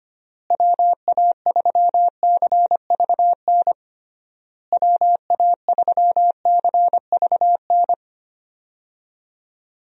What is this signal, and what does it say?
Morse code, heard as a keyed tone, about 25 wpm, 700 Hz. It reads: WA3CVN WA3CVN
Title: USA Call Signs - 6 Characters - Repeated 3x - 25wpm